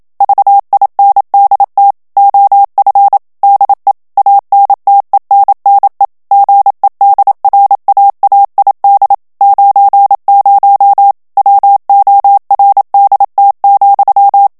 Morse code, heard as a keyed tone, about 27 wpm, 800 Hz. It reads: VINDT OF DE ANTENNE GEDRAAID 90 WORDT,
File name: HOLLANDAIS texte clair 1 vitesse 26 mots minute